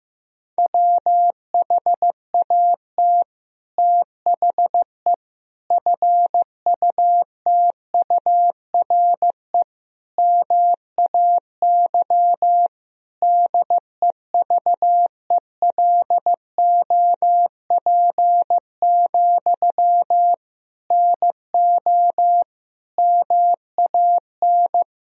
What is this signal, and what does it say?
Morse code, heard as a keyed tone, about 15 wpm, 700 Hz. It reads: WHAT THE FUTURE MAY DEVELOP, NO MAN
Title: Sentences from Top 1000 Words - Review Part 2 - 15wpm